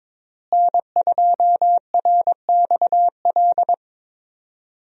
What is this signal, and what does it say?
Morse code, heard as a keyed tone, about 22 wpm, 700 Hz. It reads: N2RXL
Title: USA Call Signs - 5 Characters - Repeated 2x - 22wpm